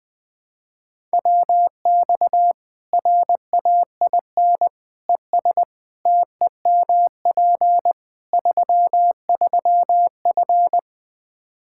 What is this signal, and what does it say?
Morse code, heard as a keyed tone, about 20 wpm, 700 Hz. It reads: WX RAIN ES TEMP 33F
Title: Practice QSO - Exchange 2 - Info - 20wpm